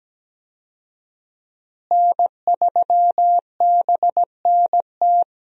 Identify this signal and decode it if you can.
Morse code, heard as a keyed tone, about 17 wpm, 700 Hz. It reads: N3BNT
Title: USA Call Signs - 5 Characters - Repeated 2x - 17wpm